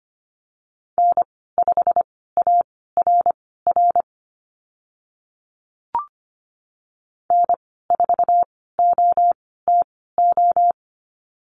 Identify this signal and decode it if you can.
Morse code, heard as a keyed tone, about 25 wpm, spaced as for 16 wpm, 690 Hz. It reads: N5ARR N4OTO
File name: Lesson 15g - Number 5 - Call Signs 25wpm